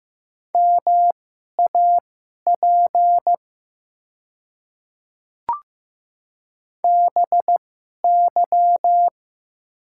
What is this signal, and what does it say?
Morse code, heard as a keyed tone, about 15 wpm, spaced as for 11 wpm, 700 Hz. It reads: MAP BY
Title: 2x Character Spacing - Top 500 Words - 15wpm